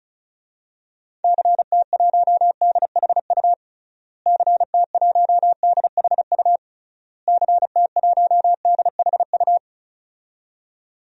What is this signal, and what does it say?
Morse code, heard as a keyed tone, about 36 wpm, 700 Hz. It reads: CT1DHU CT1DHU CT1DHU
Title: DXCC-100 Call Signs - 6 Characters - Repeated 3x - 35wpm